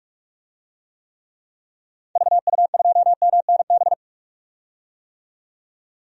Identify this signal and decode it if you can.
Morse code, heard as a keyed tone, about 45 wpm, 700 Hz. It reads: VU2MNB
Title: DXCC-100 Call Signs - 6 Characters - 45wpm